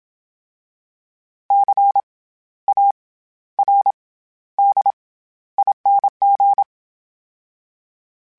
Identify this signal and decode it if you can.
Morse code, heard as a keyed tone, about 27 wpm, 800 Hz. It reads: C A R D ING
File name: Special INGs-mp3s / Special INGs-mp3s / SlowToFaster ing @25